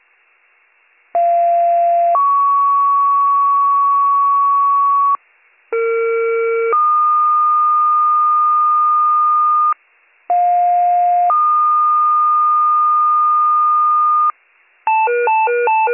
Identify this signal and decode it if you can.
Two-tone page, heard as a one-second tone, then a three-second tone, about 690 then 1120 Hz, about 460 then 1220 Hz, about 710 then 1220 Hz. It is heard from 1.1 to 14.3 s.